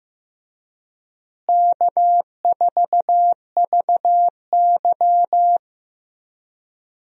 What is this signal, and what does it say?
Morse code, heard as a keyed tone, about 15 wpm, 700 Hz. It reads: K4VY
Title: USA Call Signs - 4 Characters - Repeated 2x - 15wpm